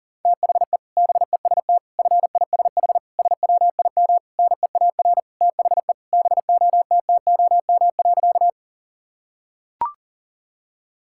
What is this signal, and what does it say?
Morse code, heard as a keyed tone, about 40 wpm, 700 Hz. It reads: THE BEST FISH SWIM NEAR THE BOTTOM.
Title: Sentences from Top 1000 Words - Review Part 6 - 40wpm